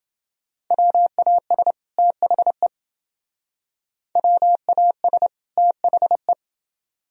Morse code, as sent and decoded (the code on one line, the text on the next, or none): .-- .- ... / - .... / .-- .- ... / - .... .
WAS THE WAS THE